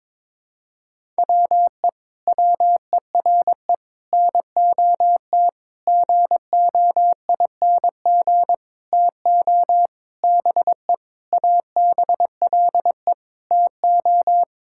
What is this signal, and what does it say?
Morse code, heard as a keyed tone, about 22 wpm, 700 Hz. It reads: WE WERE NOT GOING TO BE ABLE TO
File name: Sentences with 7-Word Phrases - 22wpm